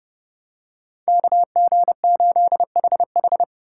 Morse code, heard as a keyed tone, about 30 wpm, 700 Hz. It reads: KG8HH
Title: USA Call Signs - 5 Characters - 2x Character Spacing - 30wpm